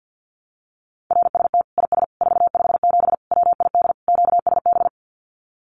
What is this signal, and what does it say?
Morse code, heard as a keyed tone, about 50 wpm, 700 Hz. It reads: RST IS 457 WID QSB
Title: Common QSO Phrases 50wpm